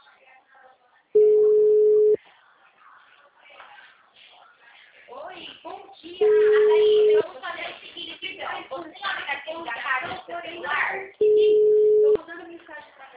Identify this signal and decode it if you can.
Ringback tone (Europe, 425 Hz), with speech.